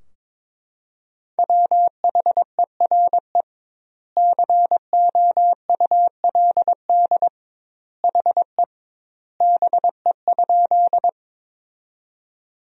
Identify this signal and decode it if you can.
Morse code, heard as a keyed tone, about 21 wpm, 700 Hz. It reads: WHERE COULD HE BE?
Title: 2x Word-Spacing - Sentences from Top 200 Words - 22wpm